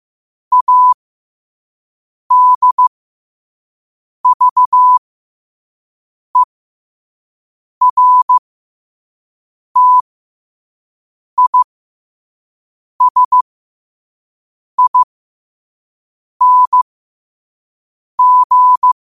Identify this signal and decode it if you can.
Morse code, heard as a keyed tone, about 15 wpm, spaced as for 5 wpm, 1000 Hz. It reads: ADVERTISING
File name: qotdfiles / 5wpm